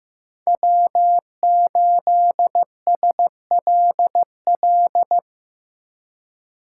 Morse code, heard as a keyed tone, about 18 wpm, 700 Hz. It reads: W8SLL
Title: Speed Racing - USA 1x3 Call Signs 15wpm